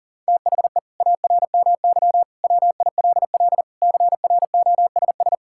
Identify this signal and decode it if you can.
Morse code, heard as a keyed tone, about 40 wpm, 700 Hz. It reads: THE ARMY WILL CROSS
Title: Sentences from Top 1000 Words - Review Part 5 - 40wpm